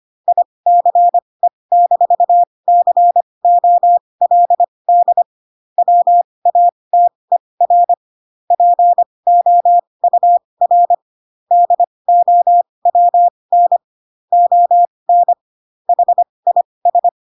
Morse code, heard as a keyed tone, about 25 wpm, spaced as for 20 wpm, 700 Hz. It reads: ICE-COLD WATER POUR DOWN ON HIS
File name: Pinocchio / Pinocchio ch6 15 25 ik1hge